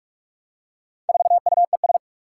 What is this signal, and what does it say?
Morse code, heard as a keyed tone, about 44 wpm, 700 Hz. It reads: 4UES
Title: Four Challenging Letter-Number Combinations 45wpm